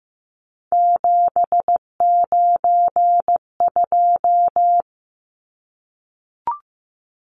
Morse code, heard as a keyed tone, about 15 wpm, 700 Hz. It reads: 792
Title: Numbers - 3-Digit 15wpm